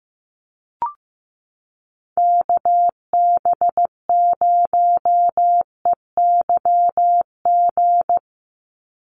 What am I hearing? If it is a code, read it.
Morse code, heard as a keyed tone, about 15 wpm, 700 Hz. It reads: KB0EYG
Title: USA Call Signs 15wpm